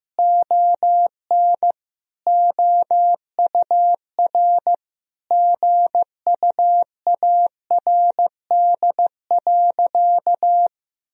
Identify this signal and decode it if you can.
Morse code, heard as a keyed tone, about 16 wpm, 700 Hz. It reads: ON OUR GUARD.
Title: Sentences from Top 1000 Words - Review Part 4 - 15wpm